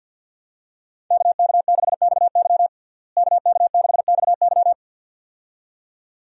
Morse code, heard as a keyed tone, about 53 wpm, 700 Hz. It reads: KK6XY KK6XY
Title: USA Call Signs - 5 Characters - Repeated 2x - 50wpm